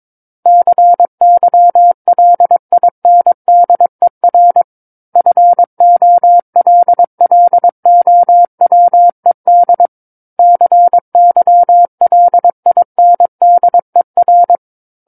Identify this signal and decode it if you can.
Morse code, heard as a keyed tone, about 22 wpm, 700 Hz. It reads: CYLINDER FOLLOWED CYLINDER
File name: Book One / 13-Chapter 13 - 22 WPM